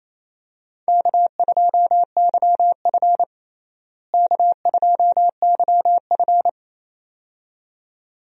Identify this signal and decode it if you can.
Morse code, heard as a keyed tone, about 28 wpm, 700 Hz. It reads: K2YF K2YF
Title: USA Call Signs - 4 Characters - Repeated 2x - 28wpm